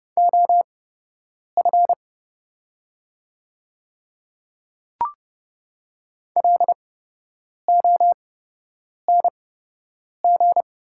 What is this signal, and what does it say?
Morse code, heard as a keyed tone, about 30 wpm, spaced as for 8 wpm, 700 Hz. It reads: OF LONG
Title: 8x Character Spacing - Top 200 Words - 30wpm